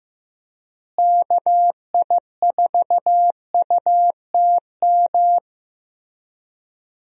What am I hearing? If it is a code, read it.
Morse code, heard as a keyed tone, about 15 wpm, 700 Hz. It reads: KI4UTM
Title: USA Call Signs - 6 Characters - Repeated 3x - 15wpm